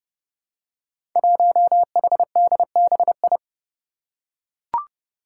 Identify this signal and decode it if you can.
Morse code, heard as a keyed tone, about 30 wpm, 700 Hz. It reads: 1HDBI
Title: Five Challenging Letter-Number Combinations 30wpm